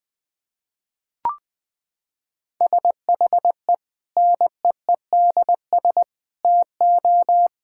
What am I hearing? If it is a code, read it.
Morse code, heard as a keyed tone, about 20 wpm, 700 Hz. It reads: SHE NEEDS TO
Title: Sentences with 4-Word Phrases - 20wpm